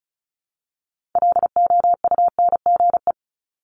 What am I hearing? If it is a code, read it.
Morse code, heard as a keyed tone, about 36 wpm, 700 Hz. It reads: LOUNGE
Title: Multi-letter Phonemes Repeated 3x 35wpm